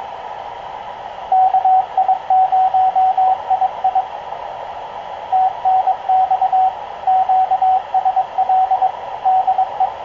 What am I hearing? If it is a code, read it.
Morse code, heard as a keyed tone, about 22 wpm, 750 Hz, with radio noise. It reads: KI0II TNX QSL DE
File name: k0frp ki0ii wa8bxn fox utc